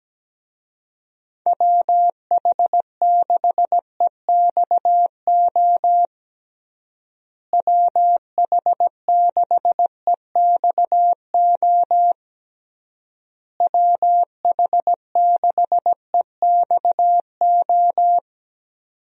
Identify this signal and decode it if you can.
Morse code, heard as a keyed tone, about 17 wpm, 700 Hz. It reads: WH6EXO WH6EXO WH6EXO
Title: USA Call Signs - 6 Characters - Repeated 3x - 17wpm